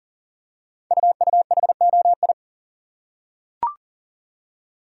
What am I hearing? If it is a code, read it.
Morse code, heard as a keyed tone, about 40 wpm, 700 Hz. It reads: UUHOI
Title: Code Groups - 5 Letters - 40wpm